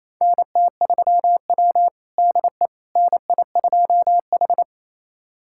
Morse code, heard as a keyed tone, about 28 wpm, 700 Hz. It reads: NT3W DE NI2H
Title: USA Callsign DE Callsign - 4 Characters - 28wpm